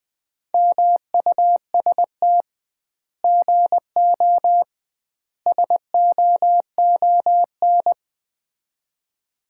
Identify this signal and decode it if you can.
Morse code, heard as a keyed tone, about 21 wpm, 700 Hz. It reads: MUST GO SOON
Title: Common QSO Phrases - 2x Word-Spacing - 20wpm